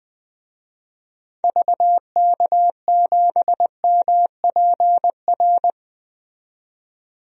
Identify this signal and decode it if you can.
Morse code, heard as a keyed tone, about 20 wpm, 700 Hz. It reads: VK7MPR
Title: DXCC-100 Call Signs - 6 Characters - 20wpm